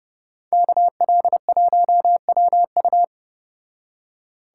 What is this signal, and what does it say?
Morse code, heard as a keyed tone, about 35 wpm, 700 Hz. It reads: KL1WU